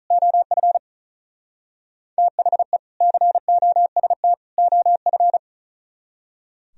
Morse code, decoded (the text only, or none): OF THE COST OF